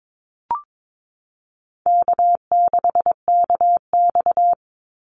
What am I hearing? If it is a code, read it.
Morse code, heard as a keyed tone, about 22 wpm, 700 Hz. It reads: K6KX
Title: Easy Call Signs Repeated 2x - 22wpm